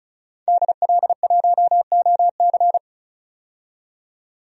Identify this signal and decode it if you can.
Morse code, heard as a keyed tone, about 34 wpm, 700 Hz. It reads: DL1OC
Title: DXCC-100 Call Signs - 5 Characters - 35wpm